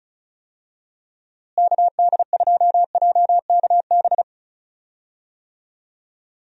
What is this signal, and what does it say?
Morse code, heard as a keyed tone, about 36 wpm, 700 Hz. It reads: KD2JKB
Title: USA Call Signs - 6 Characters - Repeated 2x - 35wpm